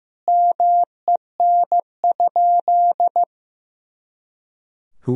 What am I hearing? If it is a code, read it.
Morse code, heard as a keyed tone, about 15 wpm, 700 Hz. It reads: MEN?
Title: Sentences from Top 200 Words and Abbreviations 15wpm